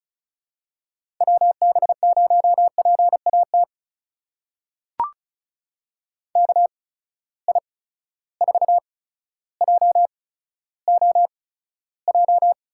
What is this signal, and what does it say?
Morse code, heard as a keyed tone, about 36 wpm, spaced as for 15 wpm, 700 Hz. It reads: WB0PAT KI4JOJ
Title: USA Call Signs - 6 Characters - 8x Character Spacing - 35wpm